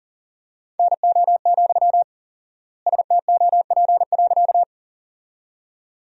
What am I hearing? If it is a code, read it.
Morse code, heard as a keyed tone, about 40 wpm, 700 Hz. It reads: NO, STOP.
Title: 4x Word-Spacing - Sentences from Top 300 Words - 40wpm